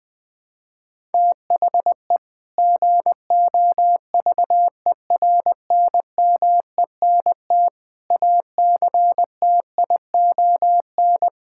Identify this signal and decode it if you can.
Morse code, heard as a keyed tone, about 20 wpm, 700 Hz. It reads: THE GOVERNMENT ACTION
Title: Sentences from Top 1000 Words - Review Part 5 - 20wpm